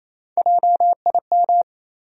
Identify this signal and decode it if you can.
Morse code, heard as a keyed tone, about 28 wpm, 700 Hz. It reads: JIM
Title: USA Common Names - Max 3 Letters - 28wpm